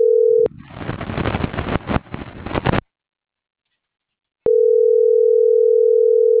Ringback tone (North America), 440 + 480 Hz.